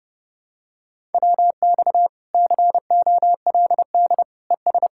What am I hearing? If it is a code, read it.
Morse code, heard as a keyed tone, about 28 wpm, 700 Hz. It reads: WX COLD ES